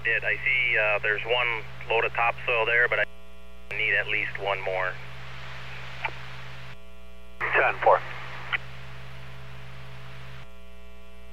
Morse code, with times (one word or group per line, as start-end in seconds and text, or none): none